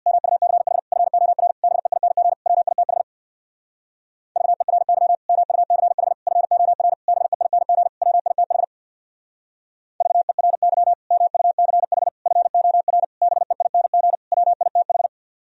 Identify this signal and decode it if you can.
Morse code, heard as a keyed tone, about 60 wpm, 700 Hz. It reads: MUCH FOR BEING WITH VERY MUCH FOR BEING WITH VERY MUCH FOR BEING WITH
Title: Speed-Racing - Sets of 5 Words 50wpm